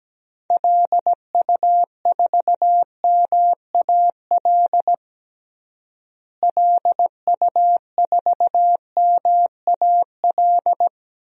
Morse code, heard as a keyed tone, about 17 wpm, 700 Hz. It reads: LU4MAL LU4MAL
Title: DXCC-100 Call Signs - 6 Characters - Repeated 3x - 17wpm